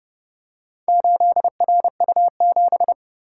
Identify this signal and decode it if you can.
Morse code, heard as a keyed tone, about 27 wpm, 700 Hz. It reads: 8RU7